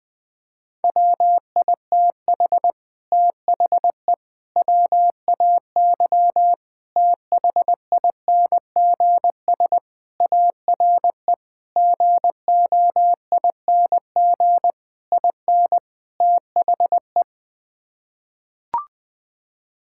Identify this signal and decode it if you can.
Morse code, heard as a keyed tone, about 20 wpm, 700 Hz. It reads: WITH THE WAY THINGS ARE GOING IN THE
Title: Sets of 8 Words - 20wpm